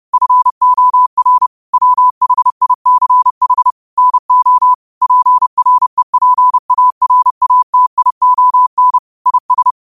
Morse code, heard as a keyed tone, about 30 wpm, 1000 Hz. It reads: FOR WHICH NO PREPARATION IS
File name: qotdfiles / 30wpm